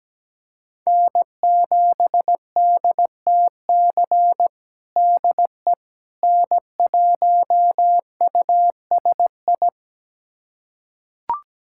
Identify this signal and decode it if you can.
Morse code, heard as a keyed tone, about 17 wpm, 700 Hz. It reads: N7DTC DE N1USI
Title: USA Callsign DE Callsign - 5 Characters - 17wpm